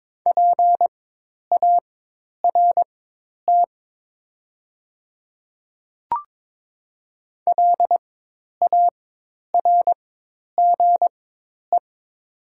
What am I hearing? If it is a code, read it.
Morse code, heard as a keyed tone, about 23 wpm, spaced as for 10 wpm, 700 Hz. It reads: PART LARGE
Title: 4x Character Spacing - Top 300 Words - 22wpm